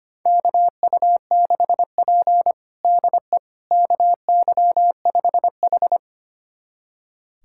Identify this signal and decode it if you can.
Morse code, heard as a keyed tone, about 25 wpm, 700 Hz. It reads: KU6P DE KY5H
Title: USA Callsign DE Callsign - 4 Characters - 25wpm